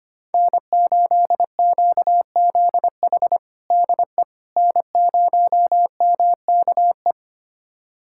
Morse code, heard as a keyed tone, about 25 wpm, 700 Hz. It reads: N8QZH DE N0MKE
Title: USA Callsign DE Callsign - 5 Characters - 25wpm